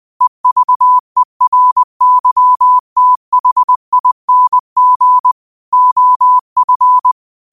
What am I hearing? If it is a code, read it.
Morse code, heard as a keyed tone, about 20 wpm, 1000 Hz. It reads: EVERYTHING OF
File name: qotdfiles / 20wpm